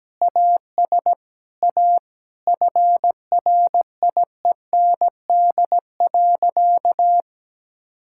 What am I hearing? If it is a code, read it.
Morse code, heard as a keyed tone, about 17 wpm, 700 Hz. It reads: AS A FRIEND.
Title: Sentences from Top 1000 Words - Review Part 6 - 17wpm